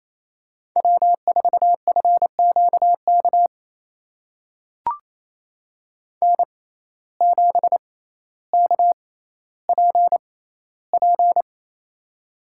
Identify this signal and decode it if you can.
Morse code, heard as a keyed tone, about 28 wpm, spaced as for 14 wpm, 700 Hz. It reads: W4FQK N7KPP
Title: USA Call Signs - 5 Characters - 6x Character Spacing - 28wpm